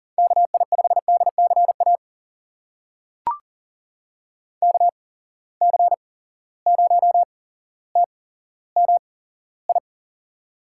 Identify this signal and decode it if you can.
Morse code, heard as a keyed tone, about 37 wpm, spaced as for 17 wpm, 700 Hz. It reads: KI5DCA KC0TMI
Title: USA Call Signs - 8x Character Spacing - 40wpm